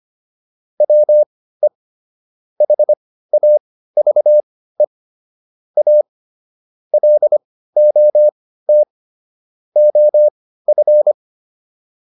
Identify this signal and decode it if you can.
Morse code, heard as a keyed tone, about 25 wpm, spaced as for 15 wpm, 600 Hz. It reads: WE HAVE A LOT OF